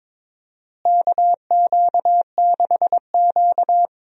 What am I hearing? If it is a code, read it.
Morse code, heard as a keyed tone, about 22 wpm, 700 Hz. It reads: KQ6Q
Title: USA Call Signs - 4 Characters - Repeated 3x - 22wpm